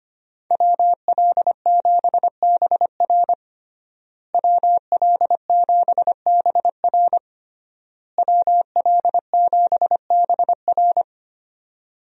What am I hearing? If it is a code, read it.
Morse code, heard as a keyed tone, about 25 wpm, 700 Hz. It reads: WL7BR WL7BR WL7BR
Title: USA Call Signs - 5 Characters - Repeated 3x - 25wpm